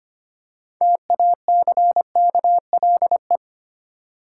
Morse code, heard as a keyed tone, about 25 wpm, 700 Hz. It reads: TACKLE
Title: Six Letter Words - 25wpm